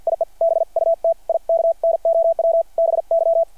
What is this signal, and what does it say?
Morse code, heard as a keyed tone, about 50 wpm, 660 Hz. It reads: IE BUT I KNOW BY
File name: archived / CRSnet